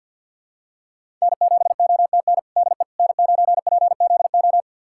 Morse code, heard as a keyed tone, about 49 wpm, 700 Hz. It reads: N7OTN DE N9PZO
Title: USA Callsign DE Callsign - 5 Characters - 50wpm